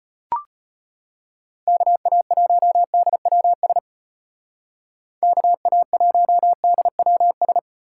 Morse code, heard as a keyed tone, about 34 wpm, 700 Hz. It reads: KA1DWS KA1DWS